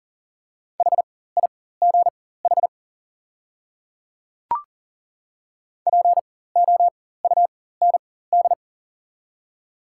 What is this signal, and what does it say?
Morse code, heard as a keyed tone, about 40 wpm, spaced as for 19 wpm, 700 Hz. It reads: HIGH POUND
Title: 4x Character Spacing - Top 500 Words - 40wpm